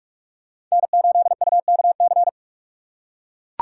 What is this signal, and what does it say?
Morse code, heard as a keyed tone, about 43 wpm, 700 Hz. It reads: N8UKC
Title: USA Call Signs - 5 Characters - 6x Character Spacing - 45wpm